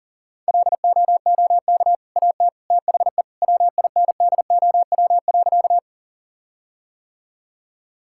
Morse code, heard as a keyed tone, about 40 wpm, 700 Hz. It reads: LOOK AT THE WINDOW.
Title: Sentences from Top 1600 Words 40wpm